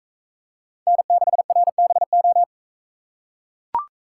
Morse code, heard as a keyed tone, about 42 wpm, 700 Hz. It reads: N6RBO